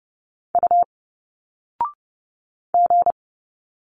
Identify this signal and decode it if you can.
Morse code, heard as a keyed tone, about 30 wpm, 700 Hz. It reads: U G
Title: Single Letters 30wpm